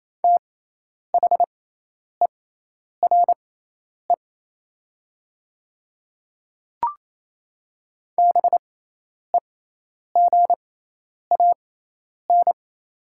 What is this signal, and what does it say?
Morse code, heard as a keyed tone, about 28 wpm, spaced as for 10 wpm, 700 Hz. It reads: THERE BEGAN
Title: 6x Character Spacing - Top 300 Words - 28wpm